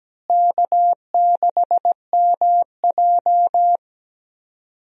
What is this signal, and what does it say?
Morse code, heard as a keyed tone, about 17 wpm, 700 Hz. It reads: K6MJ
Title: USA Call Signs - 4 Characters - Repeated 3x - 17wpm